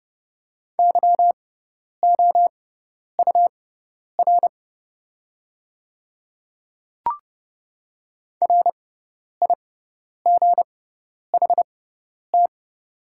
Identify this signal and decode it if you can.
Morse code, heard as a keyed tone, about 30 wpm, spaced as for 10 wpm, 700 Hz. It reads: YOUR RIGHT
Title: 6x Character Spacing - Top 300 Words - 30wpm